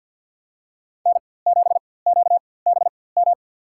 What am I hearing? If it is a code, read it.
Morse code, heard as a keyed tone, about 47 wpm, spaced as for 23 wpm, 700 Hz. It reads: N7QBM
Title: USA Call Signs - 5 Characters - 4x Character Spacing - 50wpm